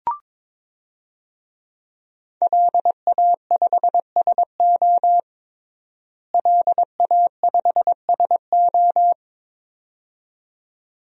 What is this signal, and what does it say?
Morse code, heard as a keyed tone, about 22 wpm, 700 Hz. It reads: LA5SO LA5SO
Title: DXCC-100 Call Signs - 5 Characters - Repeated 2x - 22wpm